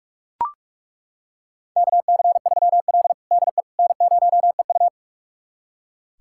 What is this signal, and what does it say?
Morse code, heard as a keyed tone, about 41 wpm, 700 Hz. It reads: KC3L DE N0EU
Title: USA Callsign DE Callsign - 4 Characters - 45wpm